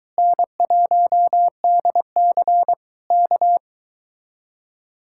Morse code, heard as a keyed tone, about 23 wpm, 700 Hz. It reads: N1DC K